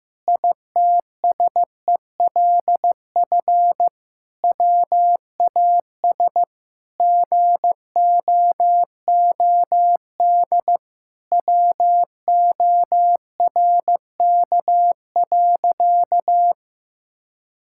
Morse code, heard as a keyed tone, about 15 wpm, 700 Hz. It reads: ITSELF WAS GOOD WORK.
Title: Sentences from Top 1000 Words - Review Part 5 - 15wpm